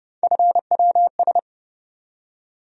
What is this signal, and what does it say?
Morse code, heard as a keyed tone, about 30 wpm, 700 Hz. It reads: FWS
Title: USA Call Sign Postfixes - 3 Letters 30wpm